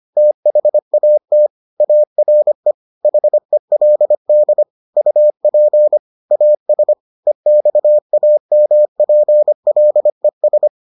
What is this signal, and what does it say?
Morse code, heard as a keyed tone, about 25 wpm, 600 Hz. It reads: THAT ARE HELD UP AS EXAMPLES